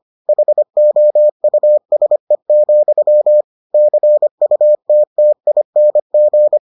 Morse code, heard as a keyed tone, about 25 wpm, 600 Hz. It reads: HOUSE, CUTTING